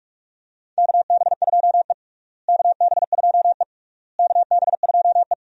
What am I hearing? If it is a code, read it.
Morse code, heard as a keyed tone, about 42 wpm, 700 Hz. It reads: KB2E KB2E KB2E
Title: USA Call Signs - 4 Characters - Repeated 3x - 45wpm